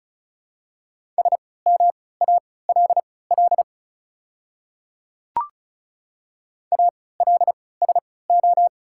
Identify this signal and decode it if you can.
Morse code, heard as a keyed tone, about 36 wpm, spaced as for 20 wpm, 700 Hz. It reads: SMALL ALSO
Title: 3x Character Spacing - Top 300 Words - 35wpm